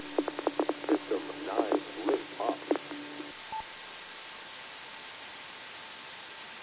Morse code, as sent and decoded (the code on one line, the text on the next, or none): none